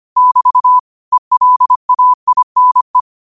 Morse code, heard as a keyed tone, about 25 wpm, 1000 Hz. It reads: <BT> ELAINE
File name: qotdfiles / 25wpm